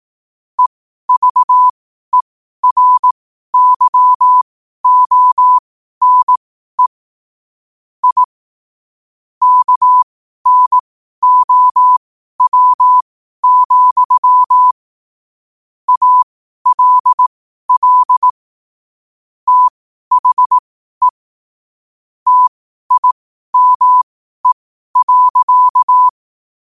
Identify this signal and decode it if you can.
Morse code, heard as a keyed tone, about 18 wpm, spaced as for 12 wpm, 1000 Hz. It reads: EVERYONE I KNOW, ALL THE TIME.
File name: qotdfiles / 12wpm